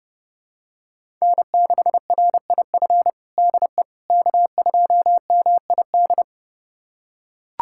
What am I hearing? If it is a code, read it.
Morse code, heard as a keyed tone, about 30 wpm, 700 Hz. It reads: N6RIF DE K2MID
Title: USA Callsign DE Callsign - 5 Characters - 30wpm